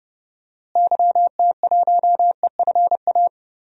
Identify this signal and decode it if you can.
Morse code, heard as a keyed tone, about 30 wpm, 700 Hz. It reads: YT1EFA
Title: DXCC-100 Call Signs - 6 Characters - Repeated 2x - 30wpm